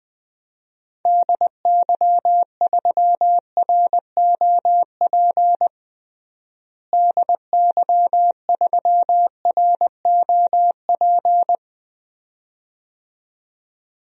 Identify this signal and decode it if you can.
Morse code, heard as a keyed tone, about 21 wpm, 700 Hz. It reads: DY3ROP DY3ROP
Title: DXCC-100 Call Signs - 6 Characters - Repeated 2x - 20wpm